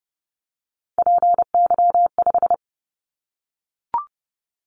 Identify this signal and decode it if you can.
Morse code, heard as a keyed tone, about 30 wpm, 700 Hz. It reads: PY5
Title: Callsign Prefixes for Top 50 DXCC - 30wpm